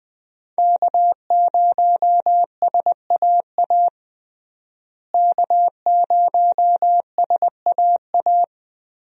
Morse code, heard as a keyed tone, about 20 wpm, 700 Hz. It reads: K0SAA K0SAA
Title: USA Call Signs - 5 Characters - Repeated 3x - 20wpm